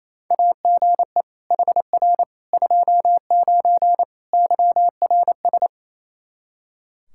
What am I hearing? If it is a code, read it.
Morse code, heard as a keyed tone, about 28 wpm, 700 Hz. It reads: AGE HR 29 YRS
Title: Practice QSO - Exchange 3 - More Info - 28wpm